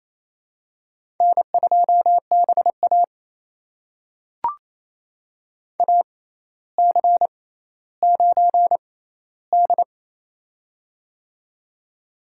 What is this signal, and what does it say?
Morse code, heard as a keyed tone, about 28 wpm, spaced as for 14 wpm, 700 Hz. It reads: N2BA AC9D
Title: USA Call Signs - 4 Characters - 6x Character Spacing - 28wpm